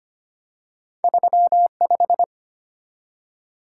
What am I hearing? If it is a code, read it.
Morse code, heard as a keyed tone, about 25 wpm, 700 Hz. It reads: 35
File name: Two Letter-Number Combinations - 25wpm